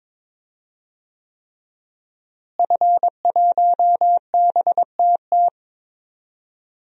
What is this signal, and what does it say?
Morse code, heard as a keyed tone, about 22 wpm, 700 Hz. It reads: F1BTT
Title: DXCC-100 Call Signs - 5 Characters - Repeated 3x - 22wpm